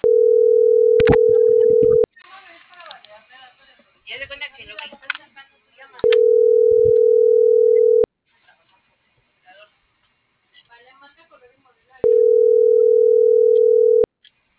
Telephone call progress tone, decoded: ringback tone

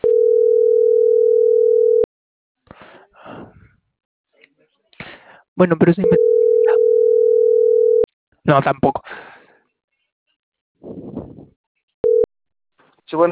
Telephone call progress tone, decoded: ringback tone

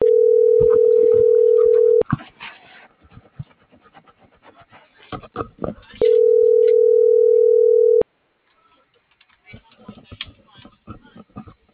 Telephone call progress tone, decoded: ringback tone

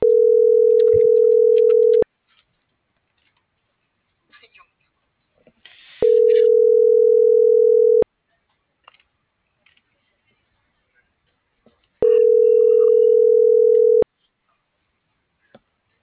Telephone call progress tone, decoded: ringback tone

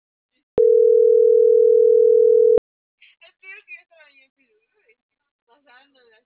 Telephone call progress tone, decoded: ringback tone

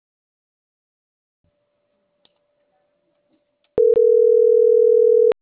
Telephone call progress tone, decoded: ringback tone

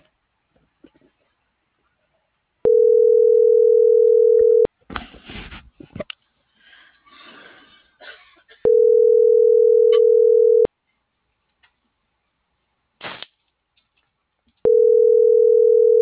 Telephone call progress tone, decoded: ringback tone

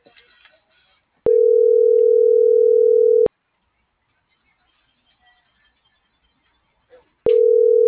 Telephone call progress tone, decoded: ringback tone

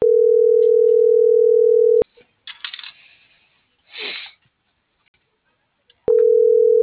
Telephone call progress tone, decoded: ringback tone

